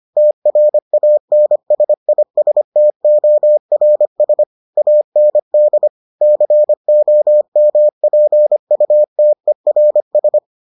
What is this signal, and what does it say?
Morse code, heard as a keyed tone, about 25 wpm, 600 Hz. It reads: TRANSISTORS AND COMPUTERS